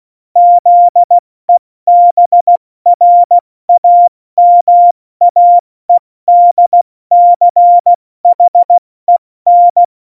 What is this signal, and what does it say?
Morse code, heard as a keyed tone, about 16 wpm, 700 Hz. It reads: ZEBRAMAEDCHEN
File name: Morsekurs / Witze / Witz 29. Dezember 2020, 70 F80